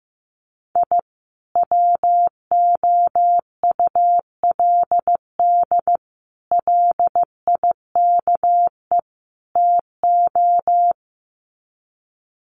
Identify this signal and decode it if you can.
Morse code, heard as a keyed tone, about 15 wpm, 700 Hz. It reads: I WOULD LIKE TO
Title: Sets of 4 Words 15wpm